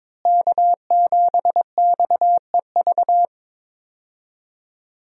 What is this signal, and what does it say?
Morse code, heard as a keyed tone, about 21 wpm, 700 Hz. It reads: K7XEV
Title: USA Call Signs - 5 Characters - Repeated 2x - 22wpm